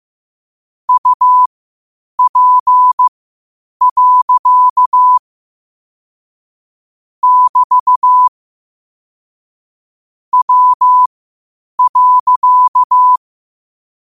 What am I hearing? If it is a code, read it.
Morse code, heard as a keyed tone, about 15 wpm, spaced as for 8 wpm, 1000 Hz. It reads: UP. <BT> W.
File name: qotdfiles / 8wpm